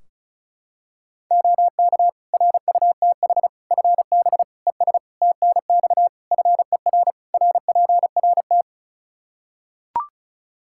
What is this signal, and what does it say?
Morse code, heard as a keyed tone, about 36 wpm, 700 Hz. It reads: OK RUTH FB ES TNX FER RPRT
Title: Practice QSO - Exchange 2 - Info - 35wpm